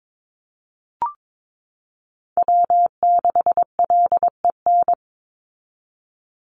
Morse code, heard as a keyed tone, about 22 wpm, 700 Hz. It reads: W6LEN
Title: Call Signs - 22wpm